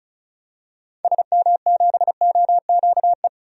Morse code, heard as a keyed tone, about 36 wpm, 700 Hz. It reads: SM7OQE
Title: DXCC-100 Call Signs - 6 Characters - 35wpm